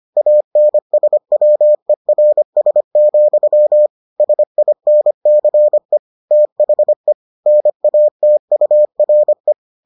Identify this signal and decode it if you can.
Morse code, heard as a keyed tone, about 25 wpm, 600 Hz. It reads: ANSWERS, SINCE THE NATURE